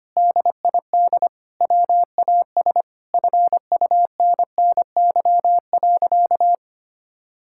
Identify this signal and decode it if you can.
Morse code, heard as a keyed tone, about 25 wpm, 690 Hz. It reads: DID WAS FUNNY.
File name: practice / 25wpm / Sentences with 2-Word Phrases - 25wpm